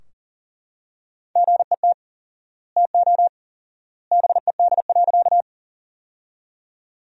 Morse code, heard as a keyed tone, about 40 wpm, 700 Hz. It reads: GET TO BED.
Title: 4x Word-Spacing - Sentences from Top 500 Words - 40wpm